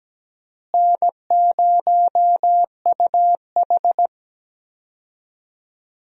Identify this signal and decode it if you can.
Morse code, heard as a keyed tone, about 17 wpm, 700 Hz. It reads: N0UH
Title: USA Call Signs - 4 Characters - Repeated 2x - 17wpm